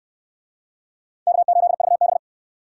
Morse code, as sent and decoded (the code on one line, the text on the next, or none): -.- ---.. ...- -...
K8VB